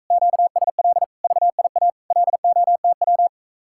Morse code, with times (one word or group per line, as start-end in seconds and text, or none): 0.1-1.0 QSL
1.2-1.9 VIA
2.1-3.3 LOTW